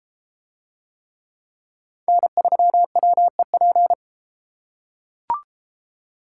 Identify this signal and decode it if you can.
Morse code, heard as a keyed tone, about 32 wpm, 700 Hz. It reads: N3WEP